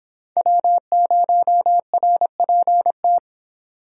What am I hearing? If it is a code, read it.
Morse code, heard as a keyed tone, about 26 wpm, 700 Hz. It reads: W0RPT